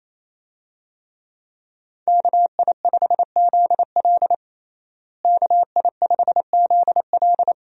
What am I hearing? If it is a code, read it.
Morse code, heard as a keyed tone, about 28 wpm, 700 Hz. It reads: KI5ZL KI5ZL
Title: USA Call Signs - 5 Characters - Repeated 2x - 28wpm